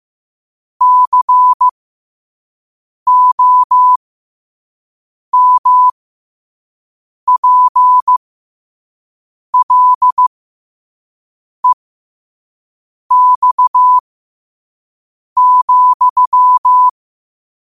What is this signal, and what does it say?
Morse code, heard as a keyed tone, about 15 wpm, spaced as for 5 wpm, 1000 Hz. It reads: COMPLEX,
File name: qotdfiles / 5wpm